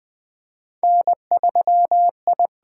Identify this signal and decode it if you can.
Morse code, heard as a keyed tone, about 21 wpm, 700 Hz. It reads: N3I
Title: USA Call Signs - 4 Characters - 8x Character Spacing - 20wpm